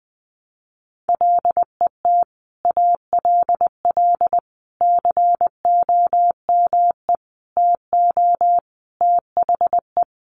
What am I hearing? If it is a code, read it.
Morse code, heard as a keyed tone, about 20 wpm, 700 Hz. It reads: LET ALL COME TO THE
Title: Sentences from Top 1100 Words 20wpm